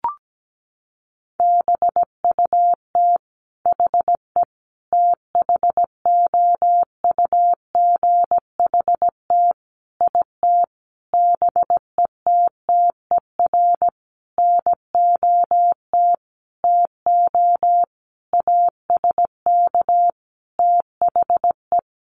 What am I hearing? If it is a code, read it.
Morse code, heard as a keyed tone, about 17 wpm, 700 Hz. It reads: BUT HE THOUGHT IT BETTER NOT TO ASK THE
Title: Sentences from Top 600 Words 17wpm